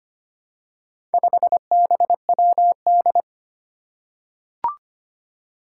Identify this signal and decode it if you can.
Morse code, heard as a keyed tone, about 25 wpm, 700 Hz. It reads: 5BWD
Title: Four Challenging Letter-Number Combinations 25wpm